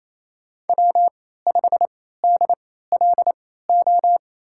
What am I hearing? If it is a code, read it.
Morse code, heard as a keyed tone, about 28 wpm, spaced as for 16 wpm, 700 Hz. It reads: W5DLO
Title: USA Call Signs - 3x Character Spacing - 28wpm